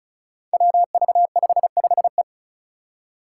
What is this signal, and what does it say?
Morse code, heard as a keyed tone, about 34 wpm, 700 Hz. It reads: WV55E
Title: Five Challenging Letter-Number Combinations 35wpm